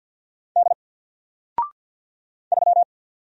Morse code, heard as a keyed tone, about 48 wpm, 700 Hz. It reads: D 3
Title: Single Letter-Number Alt 50wpm